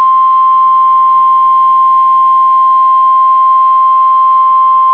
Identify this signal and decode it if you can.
Emergency Alert System (SAME) tones (the 1050 Hz attention tone). It is heard from 0.4 to 4.6 s.